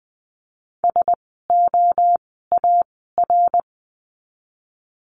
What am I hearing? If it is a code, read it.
Morse code, heard as a keyed tone, about 20 wpm, spaced as for 14 wpm, 700 Hz. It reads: SOAR
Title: 054 Lesson 10f - Letter r - Words 20wpm